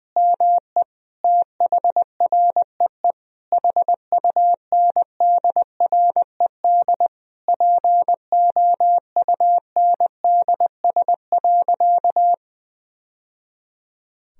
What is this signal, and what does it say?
Morse code, heard as a keyed tone, about 20 wpm, 700 Hz. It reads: ME THREE HUNDRED POUNDS.
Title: Sentences from Top 1000 Words - Review Part 2 - 20wpm